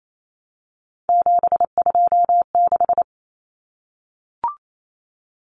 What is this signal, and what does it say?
Morse code, heard as a keyed tone, about 28 wpm, 700 Hz. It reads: 726
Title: Common Cut Numbers - 3-Digit 28wpm